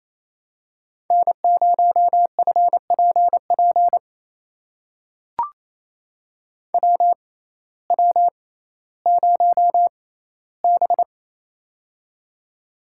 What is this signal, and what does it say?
Morse code, heard as a keyed tone, about 28 wpm, spaced as for 15 wpm, 700 Hz. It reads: N0FPP WW0B
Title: USA Call Signs - 6x Character Spacing - 28wpm